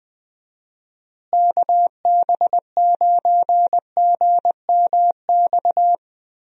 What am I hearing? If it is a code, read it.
Morse code, heard as a keyed tone, about 20 wpm, 700 Hz. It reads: KB9GMX
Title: USA Call Signs - 6 Characters - Repeated 3x - 20wpm